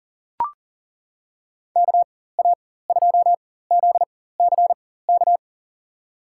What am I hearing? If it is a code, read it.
Morse code, heard as a keyed tone, about 40 wpm, spaced as for 19 wpm, 700 Hz. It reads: KA2ZCK
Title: USA Call Signs - 6 Characters - 4x Character Spacing - 40wpm